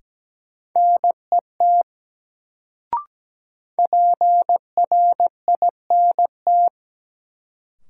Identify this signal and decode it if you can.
Morse code, heard as a keyed tone, about 17 wpm, 700 Hz. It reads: NET PRINT
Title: Sentences from Top 1400 Words - New Word Drill - 17wpm